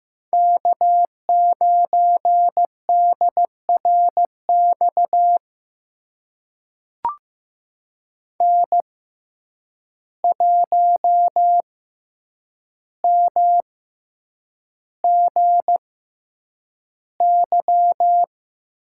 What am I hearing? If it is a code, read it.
Morse code, heard as a keyed tone, about 15 wpm, spaced as for 8 wpm, 700 Hz. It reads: K9DRX N1MGY
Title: USA Call Signs - 5 Characters - 6x Character Spacing - 15wpm